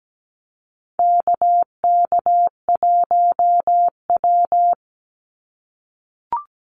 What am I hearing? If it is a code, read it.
Morse code, heard as a keyed tone, about 17 wpm, 700 Hz. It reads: KK1W
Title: NA SOTA Activator Callsigns 17wpm